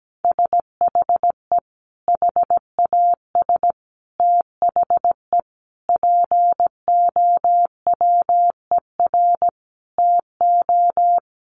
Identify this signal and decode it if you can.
Morse code, heard as a keyed tone, about 17 wpm, 700 Hz. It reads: SHE HAS THE POWER TO
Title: Sentences from Top 1200 Words 17wpm